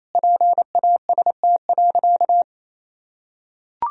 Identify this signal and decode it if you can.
Morse code, heard as a keyed tone, about 28 wpm, 700 Hz. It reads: PAST.
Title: Sentences from Top 1000 Words - Review Part 3 - 28wpm